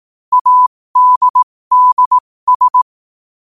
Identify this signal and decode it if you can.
Morse code, heard as a keyed tone, about 18 wpm, spaced as for 15 wpm, 1000 Hz. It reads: ADDS
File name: qotdfiles / 15wpm